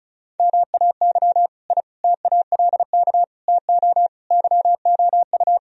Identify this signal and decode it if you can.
Morse code, heard as a keyed tone, about 36 wpm, 700 Hz. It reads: MAY I TALK TO YOU
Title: Sentences from Top 1000 Words - Review Part 4 - 35wpm